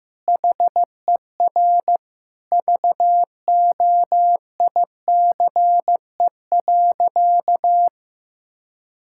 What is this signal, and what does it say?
Morse code, heard as a keyed tone, about 16 wpm, 700 Hz. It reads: HER VOICE.
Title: Sentences from Top 1600 Words 15wpm